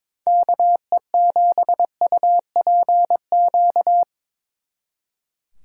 Morse code, heard as a keyed tone, about 24 wpm, 700 Hz. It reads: KE7UPQ
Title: Speed Racing - USA 2x3 Call Signs 22wpm